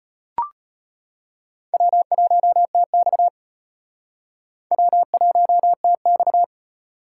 Morse code, heard as a keyed tone, about 33 wpm, 700 Hz. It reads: W1TX W1TX